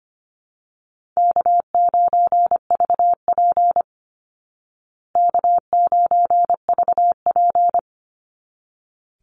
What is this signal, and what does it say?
Morse code, heard as a keyed tone, about 25 wpm, 700 Hz. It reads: K9VP K9VP
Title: Easy Call Signs Repeated 2x - 25wpm